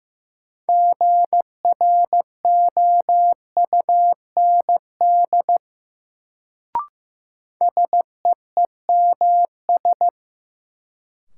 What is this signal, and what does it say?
Morse code, heard as a keyed tone, about 16 wpm, 700 Hz. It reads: GROUND SEEMS
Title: Sets of 3 Words - New Word Drill - 15wpm